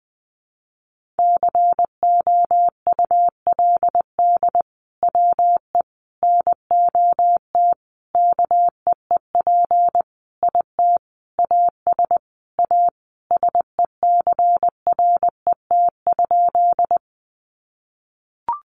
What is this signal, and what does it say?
Morse code, heard as a keyed tone, about 20 wpm, 700 Hz. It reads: COULD WE NOT KEEP IT AS A SECRET?
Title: Sentences from Top 1200 Words 20wpm